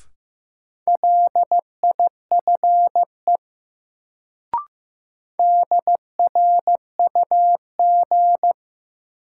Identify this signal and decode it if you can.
Morse code, heard as a keyed tone, about 15 wpm, 700 Hz. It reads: LIFE DRUG
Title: Sets of 5 Words - New Word Drill - 15wpm